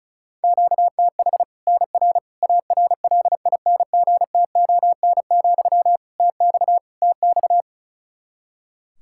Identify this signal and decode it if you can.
Morse code, heard as a keyed tone, about 34 wpm, 700 Hz. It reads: QTH NR ARLINGTON, TX TX
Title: Practice QSO - Exchange 1 - Beginning - 35wpm